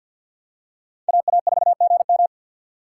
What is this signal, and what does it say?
Morse code, heard as a keyed tone, about 53 wpm, 700 Hz. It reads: AA4GM
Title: USA Call Signs - 5 Characters - 3x Character Spacing - 50wpm